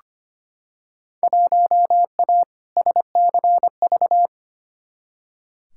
Morse code, heard as a keyed tone, about 25 wpm, 700 Hz. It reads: 1A SCV
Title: Field Day Exchanges - 25wpm